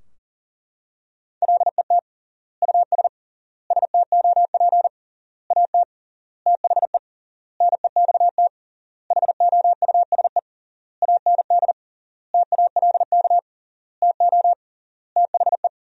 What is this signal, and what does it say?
Morse code, heard as a keyed tone, about 40 wpm, 700 Hz. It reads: LET US STOP AT THE NEXT HOUSE AND TALK TO THE
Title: 3x Word-Spacing - Sentences from Top 400 Words - 40wpm